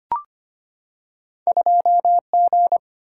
Morse code, heard as a keyed tone, about 25 wpm, 690 Hz. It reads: 2G
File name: Two Letter-Number Combinations - 25wpm